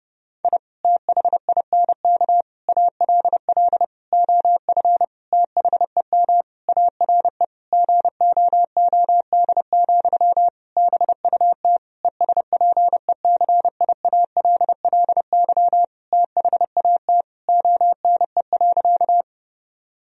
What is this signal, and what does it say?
Morse code, heard as a keyed tone, about 30 wpm, 700 Hz. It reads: I THINK ALL OF THEM ARE GOOD, BUT ESPECIALLY THAT ONE.
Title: Sentences from Top 1000 Words - Review Part 4 - 30wpm